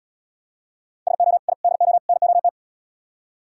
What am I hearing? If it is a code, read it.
Morse code, heard as a keyed tone, about 77 wpm, 700 Hz. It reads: SO I DO NOT